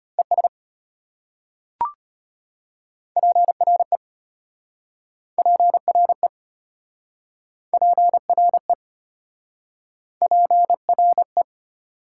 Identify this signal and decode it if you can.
Morse code, heard as a keyed tone, about 33 wpm, 700 Hz. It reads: ES PRE PRE PRE PRE